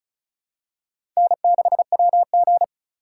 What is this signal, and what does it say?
Morse code, heard as a keyed tone, about 36 wpm, 690 Hz. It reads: N6WG
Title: USA Call Signs - 4 Characters - 2x Character Spacing - 35wpm